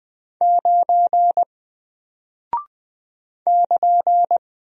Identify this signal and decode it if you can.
Morse code, heard as a keyed tone, about 20 wpm, 700 Hz. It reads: 9 <KN>
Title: Single Character (Letter:Number:Punctuation:Prosign) Alt 20wpm